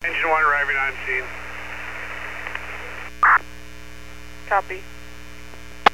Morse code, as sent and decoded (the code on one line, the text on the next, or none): none